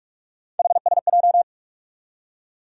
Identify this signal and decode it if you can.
Morse code, heard as a keyed tone, about 46 wpm, 700 Hz. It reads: HSJ
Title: USA Call Sign Postfixes - 3 Letters 45wpm